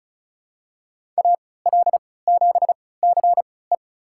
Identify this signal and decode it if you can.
Morse code, heard as a keyed tone, about 34 wpm, spaced as for 20 wpm, 700 Hz. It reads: AL7CE
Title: USA Call Signs - 5 Characters - 3x Character Spacing - 35wpm